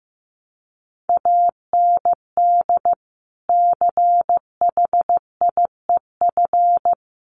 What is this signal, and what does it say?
Morse code, heard as a keyed tone, about 15 wpm, 700 Hz. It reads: AND CHIEF
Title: Sets of 5 Words Encore2 15wpm